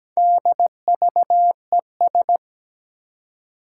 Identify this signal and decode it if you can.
Morse code, heard as a keyed tone, about 17 wpm, 700 Hz. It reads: DVES
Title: Four Challenging Letter-Number Combinations 17wpm